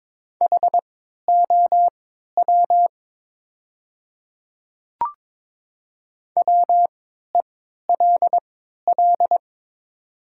Morse code, heard as a keyed tone, about 22 wpm, spaced as for 12 wpm, 700 Hz. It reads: HOW WELL
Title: 3x Character Spacing - Top 200 Words - 22wpm